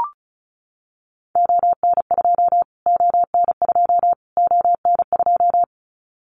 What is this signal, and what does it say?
Morse code, heard as a keyed tone, about 34 wpm, 700 Hz. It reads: ON2 ON2 ON2
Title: Callsign Prefixes for Top 25 DXCC - Repeated 3x - 35wpm